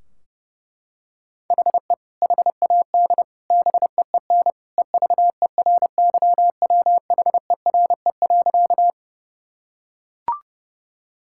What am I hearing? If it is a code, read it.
Morse code, heard as a keyed tone, about 30 wpm, 700 Hz. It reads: HE HAD BEEN EVERYWHERE.
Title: Sentences from Top 1600 Words 30wpm